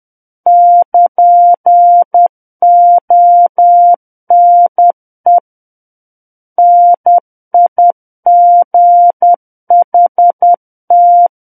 Morse code, heard as a keyed tone, about 10 wpm, 700 Hz. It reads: <KN> ONE NIGHT
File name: Book One / 01-Chapter 1 - 10 WPM